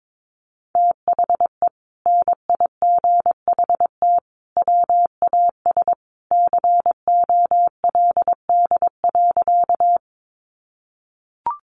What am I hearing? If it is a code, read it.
Morse code, heard as a keyed tone, about 22 wpm, 700 Hz. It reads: THE NIGHT WAS COLD.
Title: Sentences from Top 400 Words 22wpm